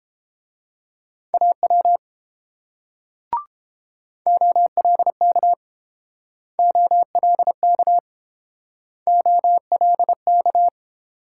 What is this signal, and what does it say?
Morse code, heard as a keyed tone, about 29 wpm, 690 Hz. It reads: AW OLK OLK OLK